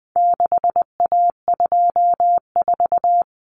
Morse code, heard as a keyed tone, about 20 wpm, 700 Hz. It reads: 6A24
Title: Common Cut Numbers - 4-Digit 20wpm